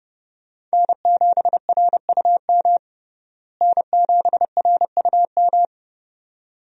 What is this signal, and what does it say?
Morse code, heard as a keyed tone, about 30 wpm, 700 Hz. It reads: N7RUM N7RUM
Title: USA Call Signs - 5 Characters - Repeated 2x - 30wpm